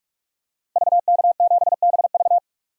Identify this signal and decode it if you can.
Morse code, heard as a keyed tone, about 46 wpm, 700 Hz. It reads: VK7BV
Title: DXCC-100 Call Signs - 5 Characters - Repeated 2x - 45wpm